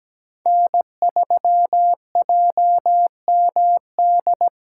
Morse code, heard as a keyed tone, about 17 wpm, 700 Hz. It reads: N3JMD
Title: USA Call Signs - 4x Character Spacing - 17wpm